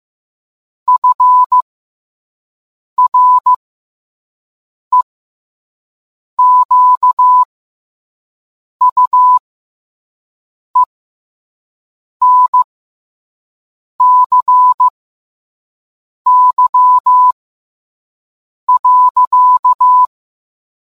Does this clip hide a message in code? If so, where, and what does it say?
Yes: Morse code from 0.9 to 20.1 s, heard as a keyed tone, about 15 wpm, spaced as for 5 wpm, 1000 Hz. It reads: FREQUENCY.